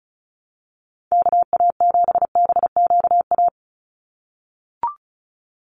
Morse code, heard as a keyed tone, about 34 wpm, 700 Hz. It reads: KA7BQA
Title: USA Call Signs 35wpm